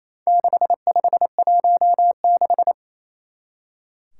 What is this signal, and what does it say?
Morse code, heard as a keyed tone, about 28 wpm, 700 Hz. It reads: 6516
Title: Four Challenging Letter-Number Combinations 28wpm